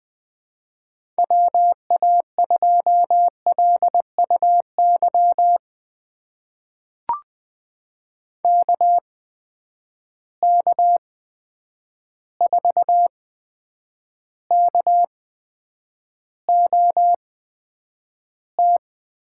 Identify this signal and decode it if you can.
Morse code, heard as a keyed tone, about 20 wpm, spaced as for 9 wpm, 700 Hz. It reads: WA2LUY KK4KOT
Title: USA Call Signs - 6 Characters - 8x Character Spacing - 20wpm